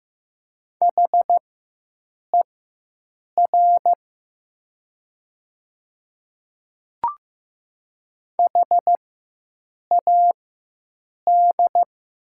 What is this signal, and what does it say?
Morse code, heard as a keyed tone, about 15 wpm, spaced as for 7 wpm, 700 Hz. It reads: HER HAD
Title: 4x Character Spacing - Top 100 Words - 15wpm